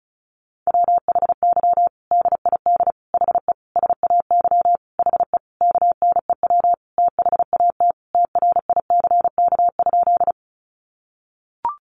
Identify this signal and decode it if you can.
Morse code, heard as a keyed tone, about 34 wpm, 700 Hz. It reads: WHY DID HE SAY HE KNEW THAT TRICK?
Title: Sentences from Top 1500 Words 35wpm